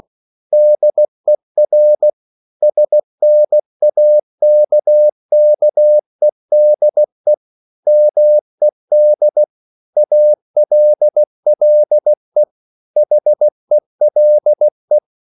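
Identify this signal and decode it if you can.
Morse code, heard as a keyed tone, about 16 wpm, 600 Hz. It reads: DER SNAKKEDE MED ALLE HELE
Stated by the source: Slow (QRS) Morse code at 16 wpm, from CW med Gnister.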